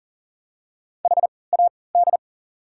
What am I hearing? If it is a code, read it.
Morse code, heard as a keyed tone, about 40 wpm, spaced as for 23 wpm, 700 Hz. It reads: HAD
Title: 3x Character Spacing - Top 200 Words - 40wpm